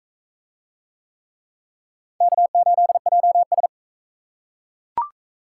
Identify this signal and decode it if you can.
Morse code, heard as a keyed tone, about 40 wpm, 700 Hz. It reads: K8JS